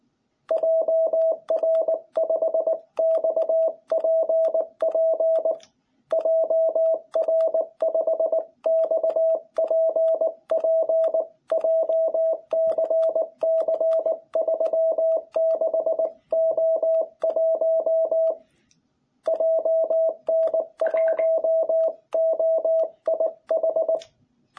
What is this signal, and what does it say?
Morse code, heard as a keyed tone, about 19 wpm, 660 Hz. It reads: JR5XPP JR5XPP JCC36O1 JN1OIH
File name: jh8cbh / 2021cw